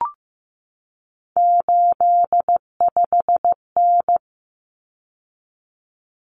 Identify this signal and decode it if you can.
Morse code, heard as a keyed tone, about 15 wpm, 700 Hz. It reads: 85N
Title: Common Cut Numbers - 3-Digit 15wpm